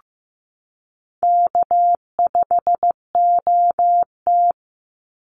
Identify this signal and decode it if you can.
Morse code, heard as a keyed tone, about 15 wpm, 700 Hz. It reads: K5OT
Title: NA SOTA Chaser Callsigns 15wpm